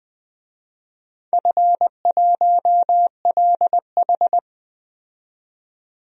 Morse code, heard as a keyed tone, about 20 wpm, 700 Hz. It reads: F1LH
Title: DXCC-100 Call Signs - 4 Characters - 20wpm